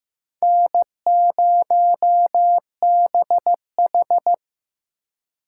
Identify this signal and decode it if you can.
Morse code, heard as a keyed tone, about 18 wpm, 700 Hz. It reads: N0BH
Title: Speed Racing - USA 1x2 Call Signs 15wpm